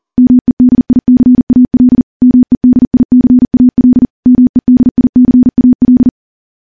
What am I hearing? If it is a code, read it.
Morse code, heard as a keyed tone, about 40 wpm, 250 Hz. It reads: MEDICAL MEDICAL MEDICAL
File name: morse / morse-training-40WPM